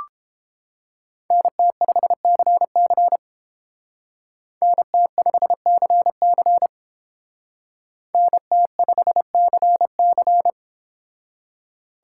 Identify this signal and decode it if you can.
Morse code, heard as a keyed tone, about 30 wpm, 700 Hz. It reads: NT5CC NT5CC NT5CC